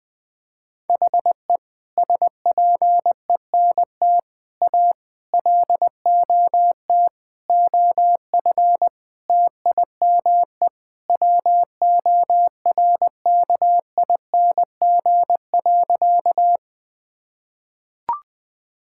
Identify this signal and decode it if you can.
Morse code, heard as a keyed tone, about 20 wpm, 700 Hz. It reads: HE SPENT A LOT OF TIME WORKING.
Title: Sentences with 5-Word Phrases - 20wpm